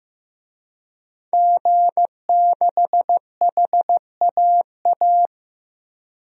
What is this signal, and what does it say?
Morse code, heard as a keyed tone, about 16 wpm, 700 Hz. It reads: G6HAA
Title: DXCC-100 Call Signs - 5 Characters - 15wpm